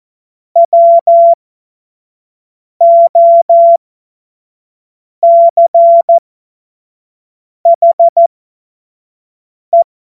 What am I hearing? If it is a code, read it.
Morse code, heard as a keyed tone, about 14 wpm, spaced as for 5 wpm, 680 Hz. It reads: WOCHE